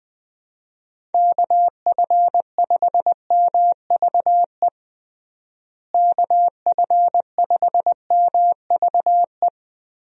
Morse code, heard as a keyed tone, about 20 wpm, 700 Hz. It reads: KF5MVE KF5MVE
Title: USA Call Signs - 6 Characters - Repeated 3x - 20wpm